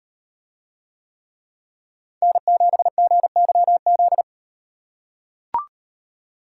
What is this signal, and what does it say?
Morse code, heard as a keyed tone, about 38 wpm, 700 Hz. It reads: N7GYZ